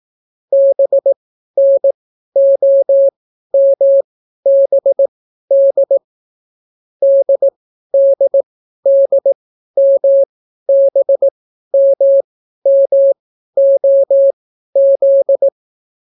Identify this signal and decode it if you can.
Morse code, heard as a keyed tone, about 18 wpm, spaced as for 12 wpm, 550 Hz. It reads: BNOMBD DDDMBMMOZ